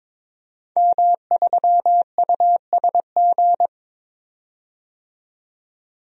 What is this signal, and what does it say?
Morse code, heard as a keyed tone, about 22 wpm, 700 Hz. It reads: M3USG
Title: DXCC-100 Call Signs - 5 Characters - 22wpm